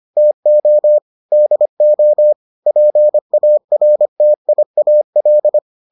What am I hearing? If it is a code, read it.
Morse code, heard as a keyed tone, about 25 wpm, 600 Hz. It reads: TO DO PARTIAL